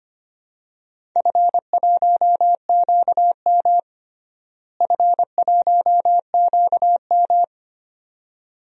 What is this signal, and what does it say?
Morse code, heard as a keyed tone, about 25 wpm, 700 Hz. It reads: F1QM F1QM
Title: DXCC-100 Call Signs - 4 Characters - Repeated 2x - 25wpm